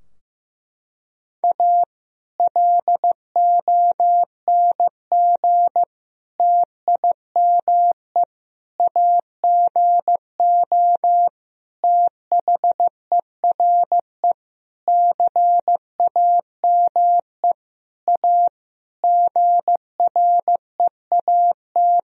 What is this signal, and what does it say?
Morse code, heard as a keyed tone, about 16 wpm, 700 Hz. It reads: A LONG TIME AGO THERE CAME A GREAT
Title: Sentences from Top 1000 Words - Review Part 3 - 15wpm